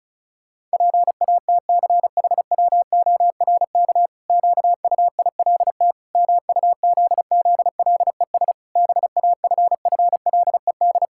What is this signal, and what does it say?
Morse code, heard as a keyed tone, about 36 wpm, 700 Hz. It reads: PATCHWORK QUILT MUZZLES BAFFLED
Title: Pangram Challenge v2 - 35wpm